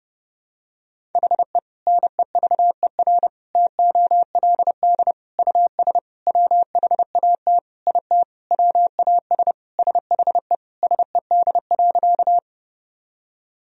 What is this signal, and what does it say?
Morse code, heard as a keyed tone, about 30 wpm, 700 Hz. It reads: HE NEVER TOLD US WHAT IT WAS SHE SED.
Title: Sentences from Top 400 Words and Abbreviations 30wpm